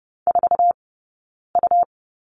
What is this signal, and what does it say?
Morse code, heard as a keyed tone, about 30 wpm, 700 Hz. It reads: 4 U
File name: Lesson 20d - Number 3 - Two Characters 30wpm